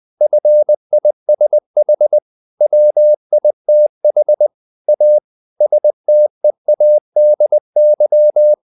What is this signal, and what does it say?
Morse code, heard as a keyed tone, about 20 wpm, 600 Hz. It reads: FISH WITH A STEADY
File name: the old man and the sea / the old man and the sea part 1 20wpm